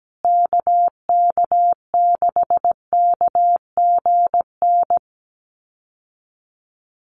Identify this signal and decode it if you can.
Morse code, heard as a keyed tone, about 17 wpm, 700 Hz. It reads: KK6KGN
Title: USA Call Signs Repeated 3x 17wpm